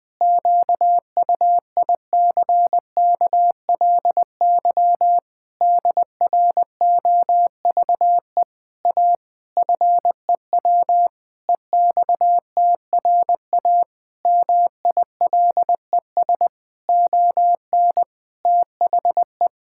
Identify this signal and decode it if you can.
Morse code, heard as a keyed tone, about 21 wpm, 700 Hz. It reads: QUICKLY DROVE A FEW EXTRA MILES ON THE
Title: Pangram Challenge v2 - 20wpm